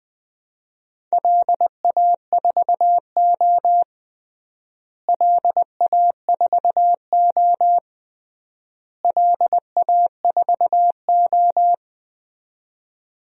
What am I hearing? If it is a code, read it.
Morse code, heard as a keyed tone, about 21 wpm, 700 Hz. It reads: LA4O LA4O LA4O
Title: DXCC-100 Call Signs - 4 Characters - Repeated 3x - 20wpm